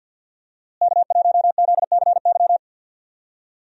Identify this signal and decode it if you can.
Morse code, heard as a keyed tone, about 51 wpm, 700 Hz. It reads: K1ZCY
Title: USA Call Signs - 5 Characters - Repeated 3x - 50wpm